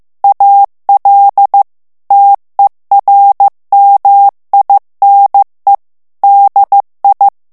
Morse code, heard as a keyed tone, about 15 wpm, 800 Hz. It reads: AL TERMINE DI
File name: morse mp3 / ITALIEN texte clair 2 vitesse 14 mots minute